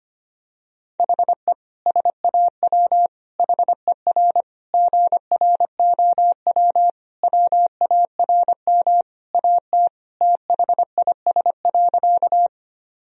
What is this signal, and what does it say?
Morse code, heard as a keyed tone, about 25 wpm, 700 Hz. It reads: HE SAW HER GROW WARM AT THIS.
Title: Sentences from Top 1000 Words - Review Part 2 - 25wpm